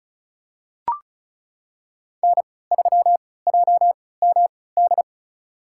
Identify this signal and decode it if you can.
Morse code, heard as a keyed tone, about 36 wpm, spaced as for 20 wpm, 700 Hz. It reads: N3JMD
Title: USA Call Signs - 5 Characters - 3x Character Spacing - 35wpm